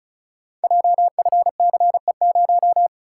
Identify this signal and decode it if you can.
Morse code, heard as a keyed tone, about 36 wpm, 700 Hz. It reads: JFCE0
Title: Code Groups - 5 Letter:Numbers - 35wpm